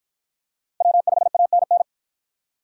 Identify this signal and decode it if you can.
Morse code, heard as a keyed tone, about 50 wpm, 700 Hz. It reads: W5ANN